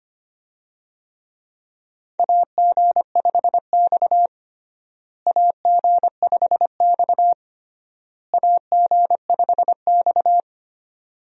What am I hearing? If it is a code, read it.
Morse code, heard as a keyed tone, about 25 wpm, 700 Hz. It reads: AG5X AG5X AG5X
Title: USA Call Signs - 4 Characters - Repeated 3x - 25wpm